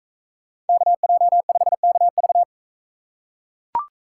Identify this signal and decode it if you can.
Morse code, heard as a keyed tone, about 41 wpm, 700 Hz. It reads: KJ5KV